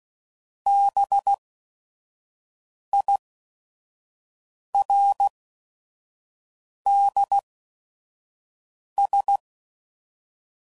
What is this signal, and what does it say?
Morse code, heard as a keyed tone, about 16 wpm, spaced as for 5 wpm, 800 Hz. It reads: BIRDS